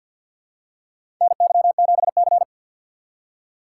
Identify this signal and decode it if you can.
Morse code, heard as a keyed tone, about 50 wpm, 700 Hz. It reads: NY7C
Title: USA Call Signs - 4 Characters - 6x Character Spacing - 50wpm